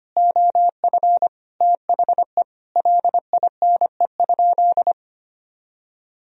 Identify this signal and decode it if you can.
Morse code, heard as a keyed tone, about 25 wpm, 700 Hz. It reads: OF THE LINE?
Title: Sentences from Top 1000 Words - Review Part 1 - 25wpm